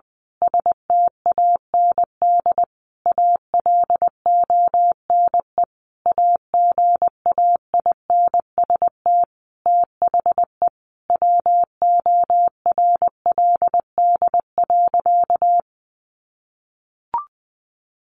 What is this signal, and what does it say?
Morse code, heard as a keyed tone, about 20 wpm, 700 Hz. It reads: STAND ALONE AGAINST THE WORLD.
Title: Sentences from Top 1400 Words 20wpm